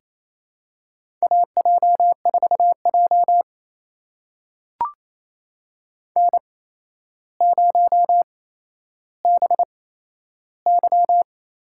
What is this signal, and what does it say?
Morse code, heard as a keyed tone, about 28 wpm, spaced as for 12 wpm, 700 Hz. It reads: AJ4J N0BY
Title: USA Call Signs - 4 Characters - 8x Character Spacing - 28wpm